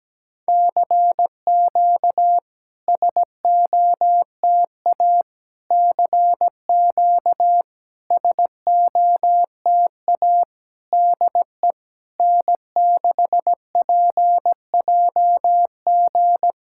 Morse code, heard as a keyed tone, about 17 wpm, 700 Hz. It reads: CQ SOTA CQ SOTA DE N6PJG
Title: SOTA QSO Practice 17wpm